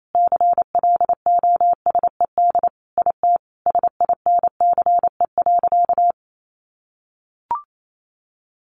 Morse code, heard as a keyed tone, about 28 wpm, 700 Hz. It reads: CLOSED IT SINCE.
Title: Sentences from Top 1400 Words 28wpm